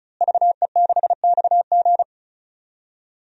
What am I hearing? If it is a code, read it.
Morse code, heard as a keyed tone, about 36 wpm, 700 Hz. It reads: VE6XG
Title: DXCC-100 Call Signs - 5 Characters - Repeated 2x - 35wpm